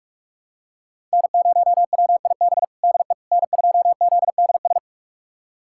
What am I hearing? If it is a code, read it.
Morse code, heard as a keyed tone, about 46 wpm, 700 Hz. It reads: N0WIB DE N2ZDS
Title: USA Callsign DE Callsign - 5 Characters - 45wpm